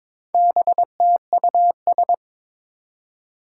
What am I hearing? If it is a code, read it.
Morse code, heard as a keyed tone, about 22 wpm, 700 Hz. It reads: BTUS
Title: Four Challenging Letter-Number Combinations 22wpm